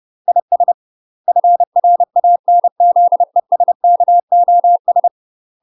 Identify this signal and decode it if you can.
Morse code, heard as a keyed tone, about 30 wpm, 700 Hz. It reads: IS FRANZESKOS
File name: UK MASTER QSOs TRAINING / 30WPM / UK MASTER QSOs 0041